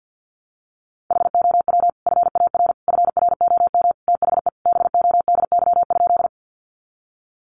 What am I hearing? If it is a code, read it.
Morse code, heard as a keyed tone, about 50 wpm, 700 Hz. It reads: HOW FAR FROM THE BODY?
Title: Sentences from Top 400 Words 50wpm